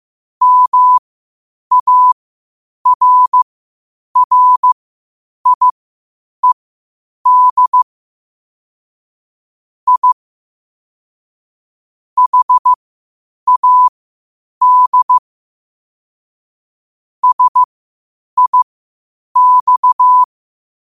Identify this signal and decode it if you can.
Morse code, heard as a keyed tone, about 15 wpm, spaced as for 8 wpm, 1000 Hz. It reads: MARRIED I HAD SIX
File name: qotdfiles / 8wpm